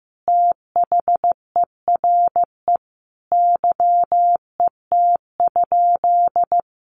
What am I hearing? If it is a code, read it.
Morse code, heard as a keyed tone, about 15 wpm, 700 Hz. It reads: THERE YET?
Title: Sentences from Top 1300 Words 15wpm